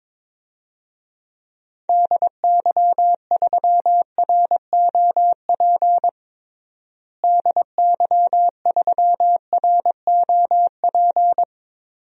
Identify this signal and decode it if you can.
Morse code, heard as a keyed tone, about 22 wpm, 700 Hz. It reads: DY3ROP DY3ROP
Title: DXCC-100 Call Signs - 6 Characters - Repeated 2x - 22wpm